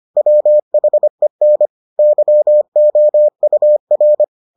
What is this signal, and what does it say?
Morse code, heard as a keyed tone, about 25 wpm, 600 Hz. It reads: WHEN YOUR